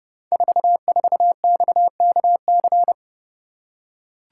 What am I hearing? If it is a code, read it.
Morse code, heard as a keyed tone, about 30 wpm, 700 Hz. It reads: 44XKC
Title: Code Groups - 5 Letter:Numbers - 30wpm